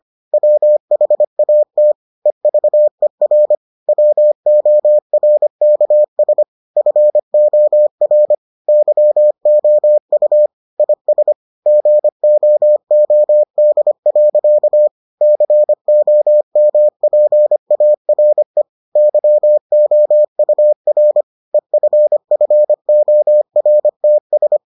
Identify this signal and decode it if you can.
Morse code, heard as a keyed tone, about 25 wpm, 600 Hz. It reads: WHAT EVER WORKS FOR YOU IS GOOD. COMPARE YOUR EFFORTS